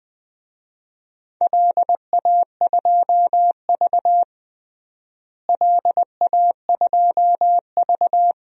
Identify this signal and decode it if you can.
Morse code, heard as a keyed tone, about 20 wpm, 700 Hz. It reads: LA2V LA2V
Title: DXCC-100 Call Signs - 4 Characters - Repeated 2x - 20wpm